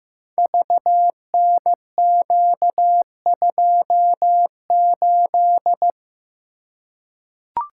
Morse code, heard as a keyed tone, about 15 wpm, 700 Hz. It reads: VNQ28
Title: Code Groups - 5 Letter:Numbers - 15wpm